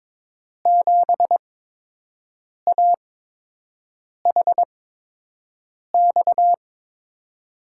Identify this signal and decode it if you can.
Morse code, heard as a keyed tone, about 21 wpm, spaced as for 6 wpm, 700 Hz. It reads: 7AHX
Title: USA Call Signs - 6 Characters - 8x Character Spacing - 22wpm